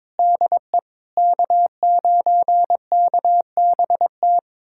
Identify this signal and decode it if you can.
Morse code, heard as a keyed tone, about 21 wpm, 700 Hz. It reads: DE K9KBT
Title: USA Callsign DE Callsign - 5 Characters - 22wpm